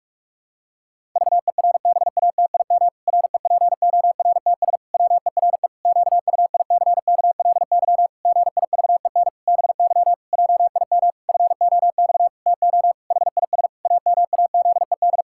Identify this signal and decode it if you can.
Morse code, heard as a keyed tone, about 41 wpm, 700 Hz. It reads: VERBATIM REPORTS WERE QUICKLY GIVEN BY JIM FOX TO HIS AMAZED
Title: Pangram Challenge v2 - 45wpm